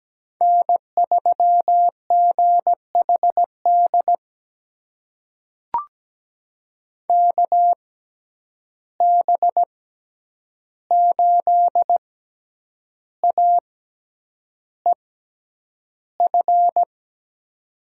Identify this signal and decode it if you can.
Morse code, heard as a keyed tone, about 17 wpm, spaced as for 8 wpm, 700 Hz. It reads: N3GHD KB8AEF
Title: USA Call Signs - 6x Character Spacing - 17wpm